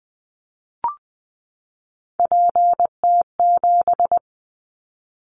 Morse code, heard as a keyed tone, about 20 wpm, 700 Hz. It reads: PT7
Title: Callsign Prefixes for Top 50 DXCC - 20wpm